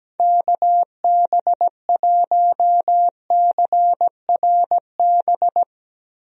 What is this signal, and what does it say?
Morse code, heard as a keyed tone, about 17 wpm, 700 Hz. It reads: KB1CRB
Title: USA Call Signs - 6 Characters - Repeated 2x - 17wpm